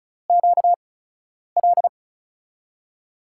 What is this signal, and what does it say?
Morse code, heard as a keyed tone, about 36 wpm, spaced as for 10 wpm, 690 Hz. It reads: QL
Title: USA Call Signs - 5 Characters - 8x Character Spacing - 35wpm